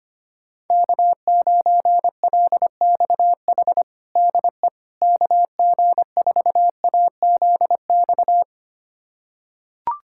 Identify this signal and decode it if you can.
Morse code, heard as a keyed tone, about 25 wpm, 700 Hz. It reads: K9LXH DE KG4AZX
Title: USA Callsign DE Callsign 25wpm